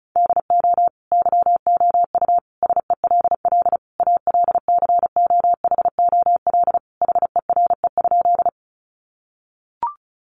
Morse code, heard as a keyed tone, about 34 wpm, 700 Hz. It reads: DO YOU SELL ALCOHOL HERE?
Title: Sentences from Top 1400 Words 35wpm